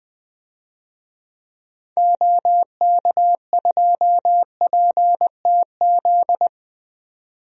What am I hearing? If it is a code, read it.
Morse code, heard as a keyed tone, about 20 wpm, 700 Hz. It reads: OK2PTZ
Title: DXCC-100 Call Signs - 6 Characters - 20wpm